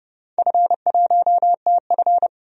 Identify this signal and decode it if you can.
Morse code, heard as a keyed tone, about 30 wpm, 700 Hz. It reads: F1TF
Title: DXCC-100 Call Signs - 4 Characters - 30wpm